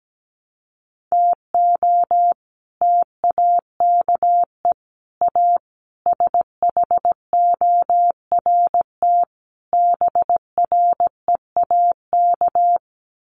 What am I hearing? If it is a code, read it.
Morse code, heard as a keyed tone, about 17 wpm, 700 Hz. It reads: TO TAKE A SHORT BREAK
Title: Sets of 5 Words Encore2 17wpm